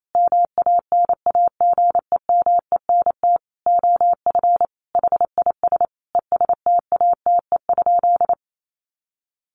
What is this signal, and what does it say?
Morse code, heard as a keyed tone, about 28 wpm, 700 Hz. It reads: MANAGEMENT OF HIS ESTATE?
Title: Sentences from Top 1200 Words 28wpm